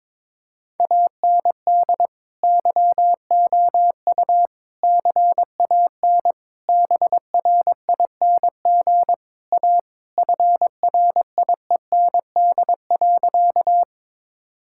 Morse code, heard as a keyed tone, about 22 wpm, 700 Hz. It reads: AND YOU CAN BRING A FRIEND.
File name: practice / 22wpm / Sentences with 2-Word Phrases - 22wpm